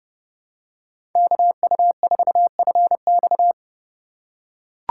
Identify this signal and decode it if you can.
Morse code, heard as a keyed tone, about 29 wpm, 700 Hz. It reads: KU4FX
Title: USA Call Signs - 5 Characters - 4x Character Spacing - 30wpm